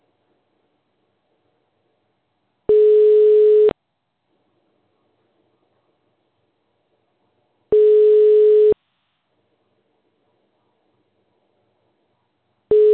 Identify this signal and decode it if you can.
Ringback tone (Europe, 425 Hz).